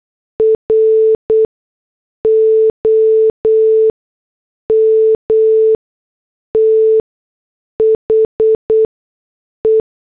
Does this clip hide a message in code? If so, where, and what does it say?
Yes: Morse code from 0.4 to 9.8 s, heard as a keyed tone, about 8 wpm, spaced as for 6 wpm, 440 Hz. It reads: ROM THE